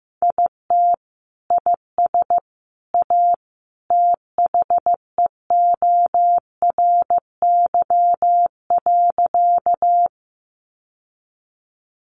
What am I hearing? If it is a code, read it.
Morse code, heard as a keyed tone, about 15 wpm, 700 Hz. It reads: IT IS A THEORY.
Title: Sentences from Top 600 Words 15wpm